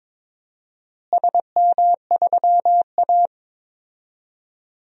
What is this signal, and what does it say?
Morse code, heard as a keyed tone, about 21 wpm, 700 Hz. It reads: SM3A
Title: DXCC-100 Call Signs - 4 Characters - Repeated 3x - 22wpm